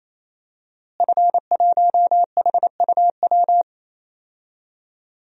Morse code, heard as a keyed tone, about 28 wpm, 700 Hz. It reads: F1HUW
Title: DXCC-100 Call Signs - 5 Characters - Repeated 3x - 28wpm